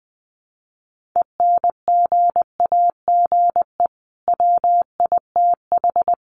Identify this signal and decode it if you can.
Morse code, heard as a keyed tone, about 20 wpm, 700 Hz. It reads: ENGAGE WITH
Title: Sentences from Top 1500 Words 20wpm